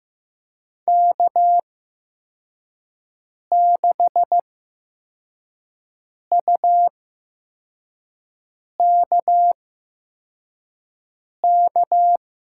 Morse code, heard as a keyed tone, about 15 wpm, spaced as for 4 wpm, 700 Hz. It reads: K6UKK
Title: USA Call Signs - 5 Characters - 8x Character Spacing - 15wpm